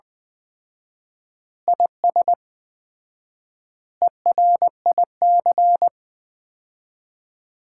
Morse code, heard as a keyed tone, about 20 wpm, 700 Hz. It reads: IS ERIC
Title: Common QSO Phrases - 4x Word-Spacing - 20wpm